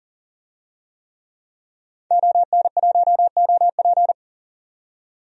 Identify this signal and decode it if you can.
Morse code, heard as a keyed tone, about 38 wpm, 700 Hz. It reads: ON1OP
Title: DXCC-100 Call Signs - 5 Characters - 40wpm